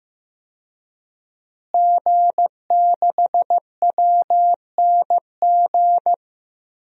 Morse code, heard as a keyed tone, about 16 wpm, 700 Hz. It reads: G6WNG
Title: DXCC-100 Call Signs - 5 Characters - Repeated 3x - 15wpm